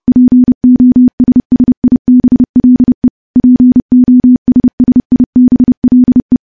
Morse code, heard as a keyed tone, about 28 wpm, 250 Hz. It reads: POSSIBLE POSSIBLE
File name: morse / morse-training-30WPM